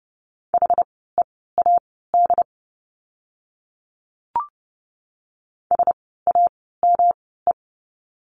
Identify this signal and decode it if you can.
Morse code, heard as a keyed tone, about 30 wpm, spaced as for 17 wpm, 700 Hz. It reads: HEAD SAME
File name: course / 30:10wpm / 108 Lesson 18f - Letter m - Words 30wpm